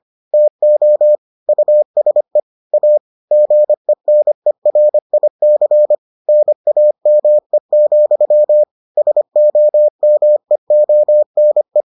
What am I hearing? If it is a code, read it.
Morse code, heard as a keyed tone, about 25 wpm, 600 Hz. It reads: TO USE A GENERIC NAME, SOMEONE